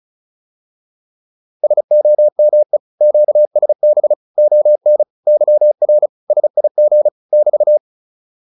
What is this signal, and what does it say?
Morse code, heard as a keyed tone, about 34 wpm, 600 Hz. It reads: SOME QSB ON YR SIG <BT>
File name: QSO407 35